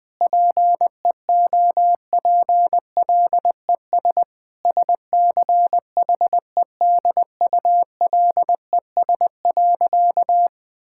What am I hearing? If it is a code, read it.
Morse code, heard as a keyed tone, about 20 wpm, 700 Hz. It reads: PEOPLES SCHEDULES.
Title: Difficult Sentences with Hints - 20wpm